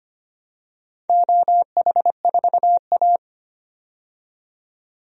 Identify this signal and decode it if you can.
Morse code, heard as a keyed tone, about 25 wpm, 700 Hz. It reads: OH4A
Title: DXCC-100 Call Signs - 4 Characters - 25wpm